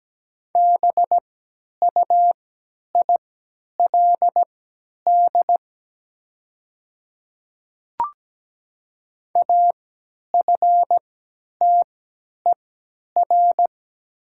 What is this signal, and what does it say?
Morse code, heard as a keyed tone, about 17 wpm, spaced as for 10 wpm, 700 Hz. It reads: BUILD AFTER
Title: 3x Character Spacing - Top 200 Words - 17wpm